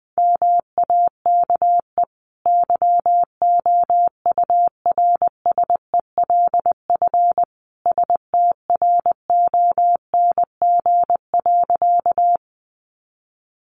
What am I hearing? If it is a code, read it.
Morse code, heard as a keyed tone, about 20 wpm, 700 Hz. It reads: MAKE YOURSELF STRONG.
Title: Sentences from Top 600 Words 20wpm